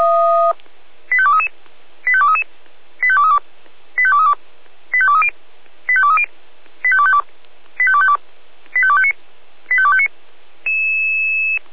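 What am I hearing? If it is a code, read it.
Five-tone selective call (ZVEI), heard as quick runs of five tones, tones of about 70 ms. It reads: 85233 85233 85232 85232 85230 85230 85252 85252 85259 85259
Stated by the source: ZVEI five-tone call received on an SDR, logged by feuerwehr-orlamuende.org.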